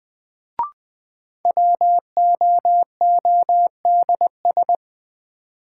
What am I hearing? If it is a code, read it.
Morse code, heard as a keyed tone, about 20 wpm, 700 Hz. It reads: WOODS
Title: Sentences from Top 1000 Words - Review Part 2 - New Word Drill - 20wpm